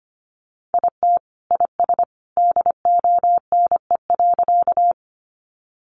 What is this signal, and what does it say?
Morse code, heard as a keyed tone, about 25 wpm, 700 Hz. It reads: IT IS DONE.
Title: Sentences from Top 500 Words 25wpm